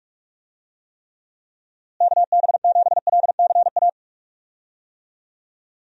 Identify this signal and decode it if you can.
Morse code, heard as a keyed tone, about 42 wpm, 700 Hz. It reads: KB7LCA
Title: USA Call Signs - 6 Characters - Repeated 2x - 45wpm